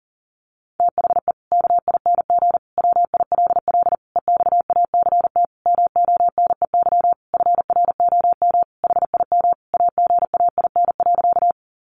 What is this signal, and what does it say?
Morse code, heard as a keyed tone, about 40 wpm, 700 Hz. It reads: THE KING WILL EXACT MONEY FROM HIM AGAIN.
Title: Sentences from Top 1300 Words 40wpm